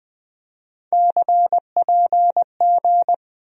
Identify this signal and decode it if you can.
Morse code, heard as a keyed tone, about 20 wpm, 700 Hz. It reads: CPG
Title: USA Call Sign Postfixes - 3 Letters 20wpm